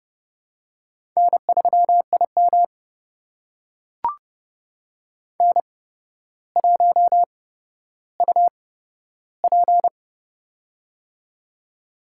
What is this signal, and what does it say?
Morse code, heard as a keyed tone, about 29 wpm, spaced as for 13 wpm, 700 Hz. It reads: N3IM N1UP
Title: USA Call Signs - 4 Characters - 8x Character Spacing - 30wpm